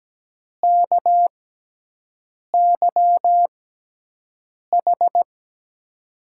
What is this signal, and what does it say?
Morse code, heard as a keyed tone, about 17 wpm, spaced as for 6 wpm, 700 Hz. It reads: KYH
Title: USA Call Signs - 6 Characters - 6x Character Spacing - 17wpm